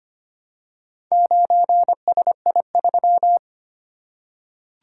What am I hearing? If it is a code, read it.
Morse code, heard as a keyed tone, about 25 wpm, 700 Hz. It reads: 9SI3
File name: practice / 25wpm / Four Letter-Number Combinations - 25wpm